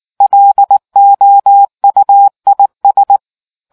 Morse code, heard as a keyed tone, about 19 wpm, 800 Hz. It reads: LOUIS